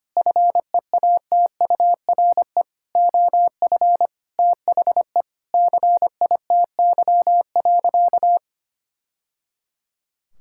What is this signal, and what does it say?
Morse code, heard as a keyed tone, about 25 wpm, 700 Hz. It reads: FEATURE OF THE CITY.
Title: Sentences from Top 1000 Words - Review Part 5 - 25wpm